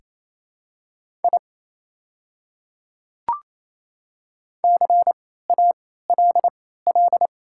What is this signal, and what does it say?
Morse code, heard as a keyed tone, about 28 wpm, spaced as for 16 wpm, 700 Hz. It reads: I CALL
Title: 3x Character Spacing - Top 200 Words - 28wpm